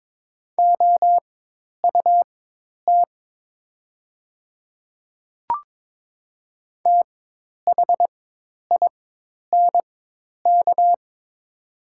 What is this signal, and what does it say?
Morse code, heard as a keyed tone, about 22 wpm, spaced as for 10 wpm, 700 Hz. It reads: OUT THINK
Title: 4x Character Spacing - Top 200 Words - 22wpm